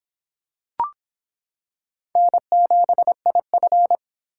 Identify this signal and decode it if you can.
Morse code, heard as a keyed tone, about 26 wpm, 700 Hz. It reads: N7IF